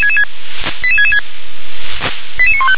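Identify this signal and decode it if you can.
Five-tone selective call (ZVEI), heard as quick runs of five tones, tones of about 70 ms. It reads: 88666 88915
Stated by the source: Five-tone ZVEI selective call 88717 received by SDR.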